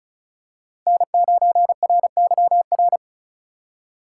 Morse code, heard as a keyed tone, about 34 wpm, 700 Hz. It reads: N9RYR
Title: USA Call Signs - 5 Characters - 8x Character Spacing - 35wpm